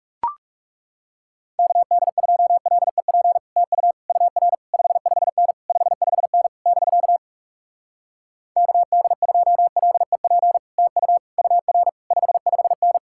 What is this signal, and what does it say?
Morse code, heard as a keyed tone, about 40 wpm, 700 Hz. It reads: KD2LEP TU UR 55N 55N <BK> KD2LEP TU UR 55N